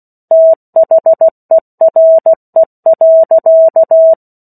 Morse code, heard as a keyed tone, about 16 wpm, 650 Hz. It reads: THERE.
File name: 03-Chapter-03 16 WPM